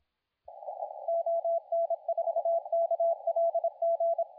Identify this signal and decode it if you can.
Morse code, heard as a keyed tone, about 27 wpm, 670 Hz, with radio noise. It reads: ON4KLG
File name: la4c / sac-cw-2017 / CONTEST AUDIO 2100Z